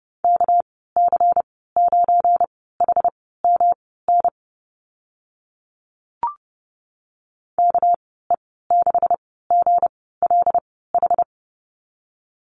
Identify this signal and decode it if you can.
Morse code, heard as a keyed tone, about 30 wpm, spaced as for 17 wpm, 700 Hz. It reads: KC9HMN KE6GLH
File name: Lesson 37g - Letter x - Call Signs 30wpm